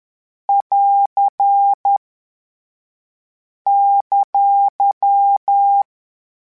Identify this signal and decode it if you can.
Morse code, heard as a keyed tone, about 10 wpm, 800 Hz. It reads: <AR> !